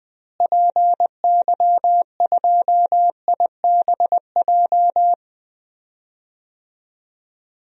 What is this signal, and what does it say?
Morse code, heard as a keyed tone, about 20 wpm, 700 Hz. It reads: PY2IBJ
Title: DXCC-100 Call Signs - 6 Characters - Repeated 3x - 20wpm